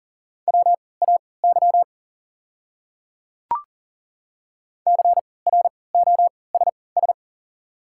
Morse code, heard as a keyed tone, about 40 wpm, spaced as for 23 wpm, 700 Hz. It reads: WAY CROSS
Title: 3x Character Spacing - Top 300 Words - 40wpm